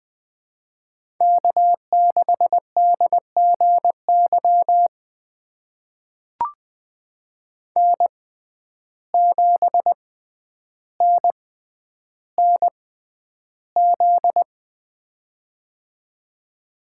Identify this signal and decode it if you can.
Morse code, heard as a keyed tone, about 20 wpm, spaced as for 10 wpm, 700 Hz. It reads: K6DGY N7NNZ
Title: USA Call Signs - 5 Characters - 6x Character Spacing - 20wpm